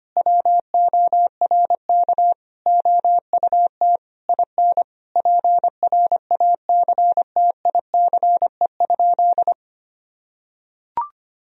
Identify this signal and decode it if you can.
Morse code, heard as a keyed tone, about 25 wpm, 700 Hz. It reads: WORK OUT IN PRACTICE?
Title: Sentences from Top 1000 Words - Review Part 4 - 25wpm